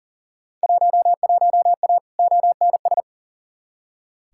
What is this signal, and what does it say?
Morse code, heard as a keyed tone, about 40 wpm, 700 Hz. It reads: 11A ONS
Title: Field Day Exchanges Encore - 40wpm